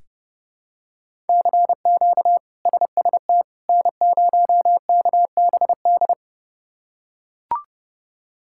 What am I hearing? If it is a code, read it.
Morse code, heard as a keyed tone, about 32 wpm, 700 Hz. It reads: CQ SST N0KBD
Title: K1USN SST QSO Practice - 30wpm